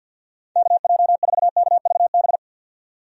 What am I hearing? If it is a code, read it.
Morse code, heard as a keyed tone, about 50 wpm, 700 Hz. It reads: KJ4KVB
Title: USA Call Signs - 6 Characters - 50wpm